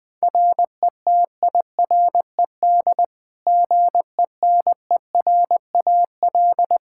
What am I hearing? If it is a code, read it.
Morse code, heard as a keyed tone, about 21 wpm, 700 Hz. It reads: RETIRED GENERAL
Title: Practice QSO - Exchange 3 - More Info - 20wpm